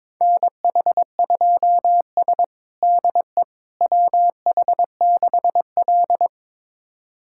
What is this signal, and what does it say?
Morse code, heard as a keyed tone, about 22 wpm, 700 Hz. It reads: NH2S DE WH6L
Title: USA Callsign DE Callsign - 4 Characters - 22wpm